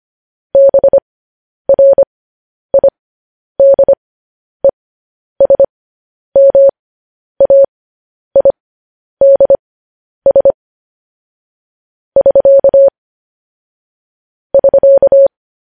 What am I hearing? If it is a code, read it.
Morse code, heard as a keyed tone, about 25 wpm, spaced as for 10 wpm, 550 Hz. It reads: BRIDESMAIDS <SK> <SK>
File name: podcast / News headlines-1x-25x10